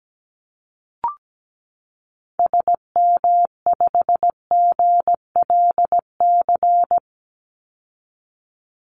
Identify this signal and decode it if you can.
Morse code, heard as a keyed tone, about 17 wpm, 700 Hz. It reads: SM5GLC
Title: Call Signs - 17wpm